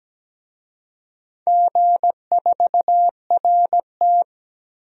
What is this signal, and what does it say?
Morse code, heard as a keyed tone, about 17 wpm, 700 Hz. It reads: G4RT
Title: DXCC-100 Call Signs - 4 Characters - Repeated 2x - 17wpm